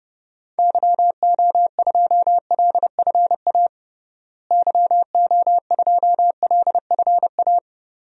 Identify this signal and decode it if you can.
Morse code, heard as a keyed tone, about 30 wpm, 700 Hz. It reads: YO2LFA YO2LFA
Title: DXCC-100 Call Signs - 6 Characters - Repeated 2x - 30wpm